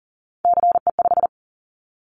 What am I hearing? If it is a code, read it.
Morse code, heard as a keyed tone, about 40 wpm, 700 Hz. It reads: CE5
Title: Callsign Prefixes for Top 100 DXCC - 40wpm